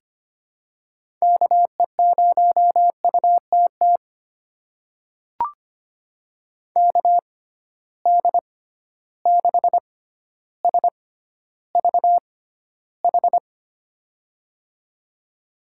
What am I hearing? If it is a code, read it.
Morse code, heard as a keyed tone, about 25 wpm, spaced as for 13 wpm, 700 Hz. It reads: KE0UTT KD6SVH
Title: USA Call Signs - 6 Characters - 6x Character Spacing - 25wpm